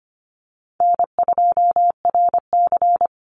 Morse code, heard as a keyed tone, about 25 wpm, 700 Hz. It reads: N2RC
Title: Easy Call Signs - 25wpm